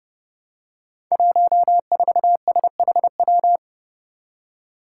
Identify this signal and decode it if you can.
Morse code, heard as a keyed tone, about 30 wpm, 700 Hz. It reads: 14SHW
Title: Five Challenging Letter-Number Combinations 30wpm